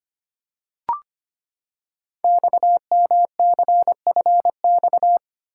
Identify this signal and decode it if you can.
Morse code, heard as a keyed tone, about 25 wpm, 700 Hz. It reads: XMCFX
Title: Code Groups - 5 Letters - 25wpm